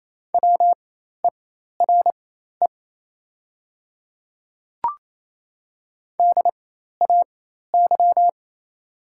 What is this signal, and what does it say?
Morse code, heard as a keyed tone, about 28 wpm, spaced as for 13 wpm, 700 Hz. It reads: WERE DAY
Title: 4x Character Spacing - Top 100 Words - 28wpm